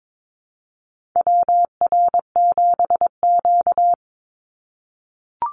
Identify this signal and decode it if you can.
Morse code, heard as a keyed tone, about 22 wpm, 700 Hz. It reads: WR7Q
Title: Call Signs - 22wpm